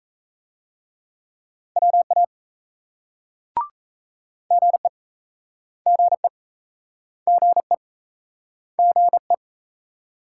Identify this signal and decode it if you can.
Morse code, heard as a keyed tone, about 38 wpm, 700 Hz. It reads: WA GE GE GE GE